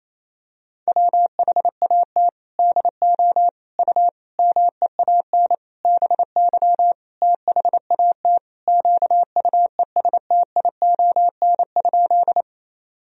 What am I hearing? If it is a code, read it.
Morse code, heard as a keyed tone, about 28 wpm, 700 Hz. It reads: WHAT DO U MEAN BY THAT QUESTION?
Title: Sentences from Top 400 Words and Abbreviations 28wpm